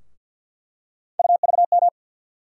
Morse code, heard as a keyed tone, about 53 wpm, 700 Hz. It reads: UVM
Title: USA Call Sign Postfixes - 3 Letters 50wpm